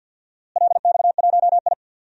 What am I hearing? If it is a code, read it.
Morse code, heard as a keyed tone, about 53 wpm, 700 Hz. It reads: LX1I
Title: DXCC-100 Call Signs - 4 Characters - Repeated 2x - 50wpm